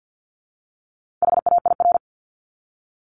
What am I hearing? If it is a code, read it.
Morse code, heard as a keyed tone, about 50 wpm, 700 Hz. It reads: HAIR
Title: Top 1751-2000 Words 50wpm